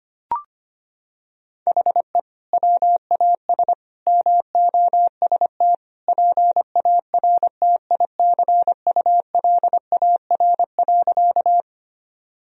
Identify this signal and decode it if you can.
Morse code, heard as a keyed tone, about 25 wpm, 700 Hz. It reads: HE WAS MOST PARTICULAR.
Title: Sentences from Top 1000 Words - Review Part 4 - 25wpm